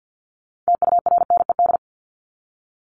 Morse code, heard as a keyed tone, about 50 wpm, 700 Hz. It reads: TURNED
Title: ED Suffix - Top 100 Words with Base Words 50wpm